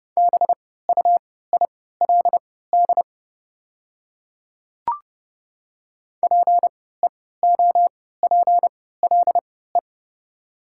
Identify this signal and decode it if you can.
Morse code, heard as a keyed tone, about 30 wpm, spaced as for 17 wpm, 700 Hz. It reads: BUILD PEOPLE
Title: 3x Character Spacing - Top 300 Words - 30wpm